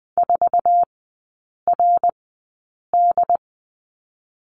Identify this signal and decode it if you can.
Morse code, heard as a keyed tone, about 19 wpm, spaced as for 8 wpm, 700 Hz. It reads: 4RD
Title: 074 Lesson 13e - Letter L - Three Characters 20wpm